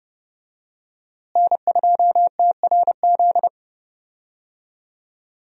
Morse code, heard as a keyed tone, about 31 wpm, 700 Hz. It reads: N2TRZ
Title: USA Call Signs - 5 Characters - Repeated 2x - 30wpm